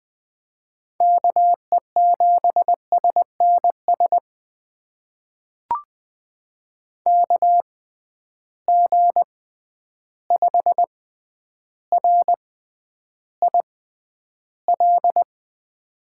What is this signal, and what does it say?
Morse code, heard as a keyed tone, about 20 wpm, spaced as for 10 wpm, 700 Hz. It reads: KE7SNS KG5RIL
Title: USA Call Signs - 6 Characters - 6x Character Spacing - 20wpm